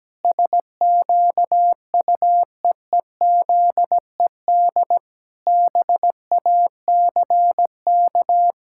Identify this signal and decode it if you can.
Morse code, heard as a keyed tone, about 17 wpm, 700 Hz. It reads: SQUEEZED BACK
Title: Pangram Challenge v2 - 17wpm